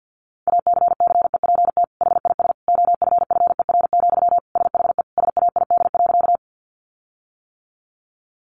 Morse code, heard as a keyed tone, about 50 wpm, 700 Hz. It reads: ACCEPT HIS OFFER, SHE SAID.
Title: Sentences from Top 1300 Words 50wpm